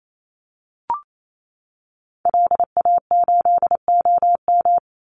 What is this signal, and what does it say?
Morse code, heard as a keyed tone, about 28 wpm, 700 Hz. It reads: LA8OM
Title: Call Signs - 28wpm